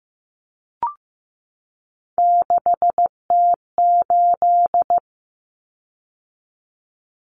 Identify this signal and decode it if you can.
Morse code, heard as a keyed tone, about 15 wpm, 700 Hz. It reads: 6T8
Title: Common Cut Numbers - 3-Digit 15wpm